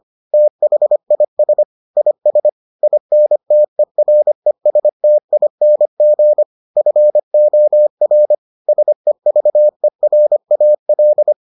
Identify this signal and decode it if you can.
Morse code, heard as a keyed tone, about 25 wpm, 600 Hz. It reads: THIS IS INTERESTING FOR SEVERAL